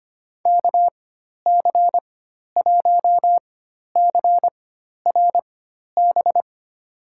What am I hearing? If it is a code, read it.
Morse code, heard as a keyed tone, about 25 wpm, spaced as for 12 wpm, 700 Hz. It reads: KC1CRB
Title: USA Call Signs - 6 Characters - 4x Character Spacing - 25wpm